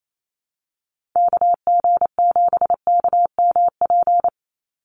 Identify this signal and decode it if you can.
Morse code, heard as a keyed tone, about 28 wpm, 700 Hz. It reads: KG7KMP
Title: USA Call Signs 28wpm